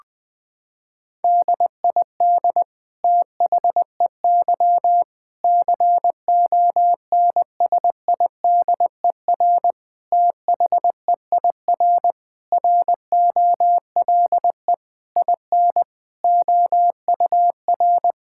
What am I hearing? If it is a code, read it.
Morse code, heard as a keyed tone, about 20 wpm, 700 Hz. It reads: DID THEY CONSIDER THEIR ROLE IN OUR
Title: Sentences from Top 1000 Words - Review Part 6 - 20wpm